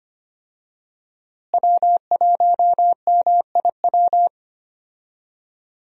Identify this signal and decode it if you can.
Morse code, heard as a keyed tone, about 25 wpm, 700 Hz. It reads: W1MIW
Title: USA Call Signs - 5 Characters - Repeated 3x - 25wpm